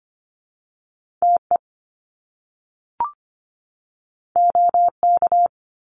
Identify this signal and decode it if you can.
Morse code, heard as a keyed tone, about 25 wpm, 700 Hz. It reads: TE OK
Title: Top Two Letter Combinations in Words 25wpm